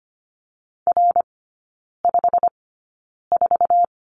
Morse code, heard as a keyed tone, about 25 wpm, spaced as for 9 wpm, 690 Hz. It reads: R54
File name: Lesson 15e - Number 5 - Three Characters 25wpm